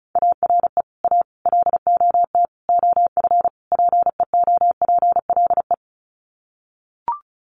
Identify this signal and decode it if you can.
Morse code, heard as a keyed tone, about 34 wpm, 700 Hz. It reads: ARE A LOT OF PEOPLE
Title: Sets of 5 Words 35wpm